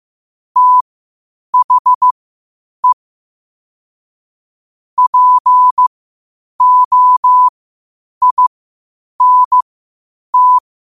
Morse code, heard as a keyed tone, about 15 wpm, spaced as for 8 wpm, 1000 Hz. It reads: THE POINT